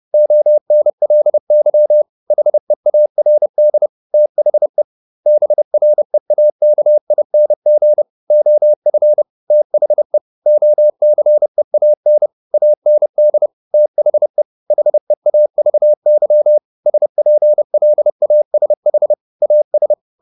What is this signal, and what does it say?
Morse code, heard as a keyed tone, about 30 wpm, 600 Hz. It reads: ONLY HEARD THE BREAKING OF THE OCEAN AND THE HEAVY SPLASH AS